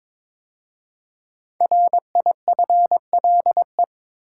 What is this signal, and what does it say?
Morse code, heard as a keyed tone, about 22 wpm, 700 Hz. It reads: RIFLE
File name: 5 Letter Words - First 3 Letters Given - 22wpm